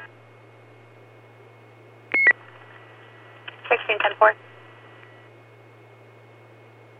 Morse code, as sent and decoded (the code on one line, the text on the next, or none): none